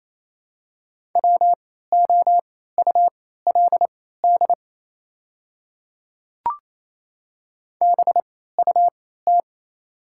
Morse code, heard as a keyed tone, about 28 wpm, spaced as for 16 wpm, 700 Hz. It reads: WOULD BUT
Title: 3x Character Spacing - Top 100 Words - 28wpm